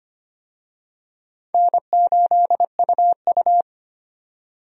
Morse code, heard as a keyed tone, about 25 wpm, 700 Hz. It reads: N8UU
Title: USA Call Signs - 4 Characters - 25wpm